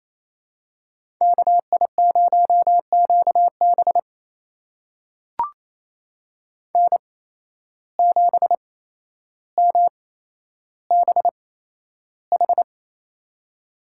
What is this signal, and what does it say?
Morse code, heard as a keyed tone, about 28 wpm, spaced as for 12 wpm, 700 Hz. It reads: KI0QB N7MBH
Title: USA Call Signs - 5 Characters - 8x Character Spacing - 28wpm